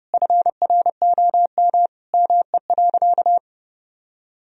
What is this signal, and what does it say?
Morse code, heard as a keyed tone, about 30 wpm, 700 Hz. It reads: FROM ME.
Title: Sentences from Top 1000 Words - Review Part 1 - 30wpm